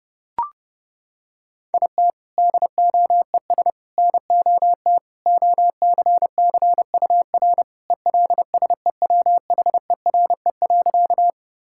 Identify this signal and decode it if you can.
Morse code, heard as a keyed tone, about 30 wpm, 700 Hz. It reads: IT DOES NOT OCCUR ELSEWHERE.
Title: Sentences from Top 1600 Words 30wpm